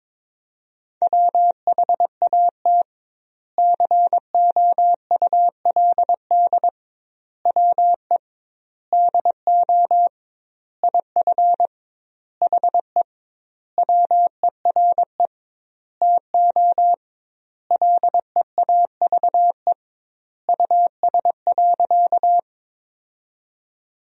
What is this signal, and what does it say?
Morse code, heard as a keyed tone, about 22 wpm, 700 Hz. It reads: WHAT COULD WE DO IF HE WERE TO LEAVE US.
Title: 2x Word-Spacing - Sentences from Top 400 Words - 22wpm